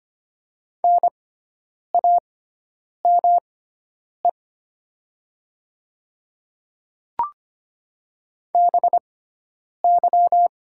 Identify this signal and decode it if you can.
Morse code, heard as a keyed tone, about 25 wpm, spaced as for 9 wpm, 690 Hz. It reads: NAME BY
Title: 6x Character Spacing - Top 200 Words - 25wpm